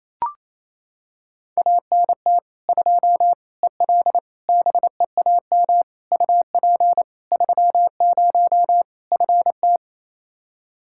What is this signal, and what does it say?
Morse code, heard as a keyed tone, about 28 wpm, 700 Hz. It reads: ANT 2 EL BEAM UP 30 FT
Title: K1USN SST QSO Practice 28wpm